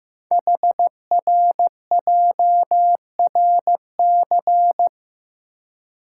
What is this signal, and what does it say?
Morse code, heard as a keyed tone, about 15 wpm, 700 Hz. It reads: HRJRC
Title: Code Groups - 5 Letters - 15wpm